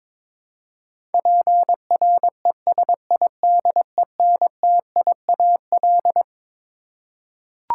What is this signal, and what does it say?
Morse code, heard as a keyed tone, about 22 wpm, 700 Hz. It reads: PRESIDENTIAL
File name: Twelve Letter Words - 22wpm